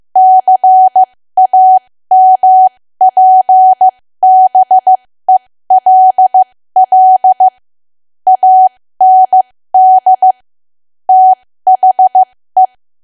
Morse code, heard as a keyed tone, about 15 wpm, spaced as for 13 wpm, 750 Hz. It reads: CAMPBELL AND THE
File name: cwa / W1AW-Level2 / 13WPM